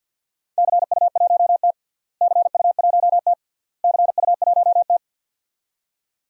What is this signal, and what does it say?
Morse code, heard as a keyed tone, about 48 wpm, 700 Hz. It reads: CU1T CU1T CU1T
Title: DXCC-100 Call Signs - 4 Characters - Repeated 3x - 50wpm